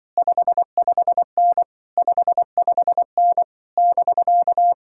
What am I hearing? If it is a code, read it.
Morse code, heard as a keyed tone, about 25 wpm, 700 Hz. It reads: 55N 55N <BK>
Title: Speed-Racing Common QSO Phrases - 20wpm